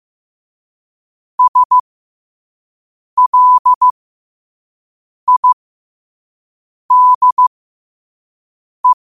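Morse code, heard as a keyed tone, about 15 wpm, spaced as for 5 wpm, 1000 Hz. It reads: SLIDE
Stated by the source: Morse code quote practice at 5 words per minute.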